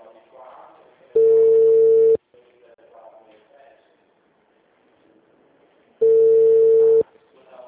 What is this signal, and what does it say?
Ringback tone (China, 450 Hz), with speech.